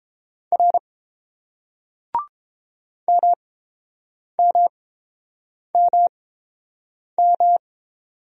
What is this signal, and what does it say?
Morse code, heard as a keyed tone, about 30 wpm, spaced as for 7 wpm, 700 Hz. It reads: R MMMM